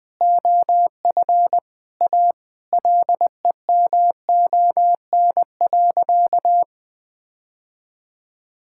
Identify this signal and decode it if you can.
Morse code, heard as a keyed tone, about 20 wpm, 700 Hz. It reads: OF A LEMON.
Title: Sentences with 5-Word Phrases - 20wpm